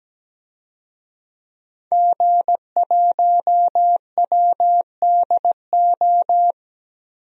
Morse code, heard as a keyed tone, about 17 wpm, 700 Hz. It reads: G1WDO
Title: DXCC-100 Call Signs - 5 Characters - Repeated 2x - 17wpm